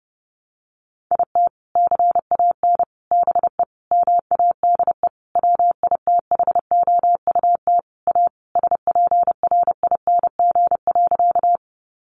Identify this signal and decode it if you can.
Morse code, heard as a keyed tone, about 30 wpm, 700 Hz. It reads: IT CAN BE MADE WITHOUT A SPRING.
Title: Sentences from Top 1200 Words 30wpm